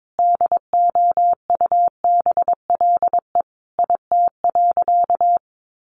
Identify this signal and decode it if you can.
Morse code, heard as a keyed tone, about 22 wpm, 700 Hz. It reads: DOUBLE IT.
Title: Sentences from Top 1400 Words 22wpm